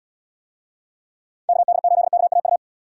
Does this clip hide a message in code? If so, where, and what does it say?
Yes: Morse code from 1.5 to 2.6 s, heard as a keyed tone, about 69 wpm, 700 Hz. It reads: KD8KAU